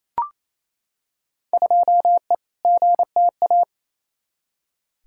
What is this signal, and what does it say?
Morse code, heard as a keyed tone, about 28 wpm, 700 Hz. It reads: 2E GTA
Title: Field Day Exchanges Encore - 28wpm